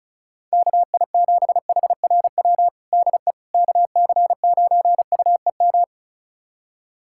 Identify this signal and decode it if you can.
Morse code, heard as a keyed tone, about 36 wpm, 700 Hz. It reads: KI7HRW DE KC9UEM
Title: USA Callsign DE Callsign - 6 Characters - 35wpm